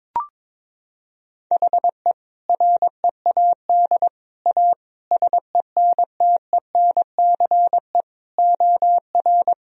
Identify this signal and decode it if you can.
Morse code, heard as a keyed tone, about 22 wpm, 700 Hz. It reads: HE READ A SENTENCE OR
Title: Sentences from Top 1000 Words - Review Part 1 - 22wpm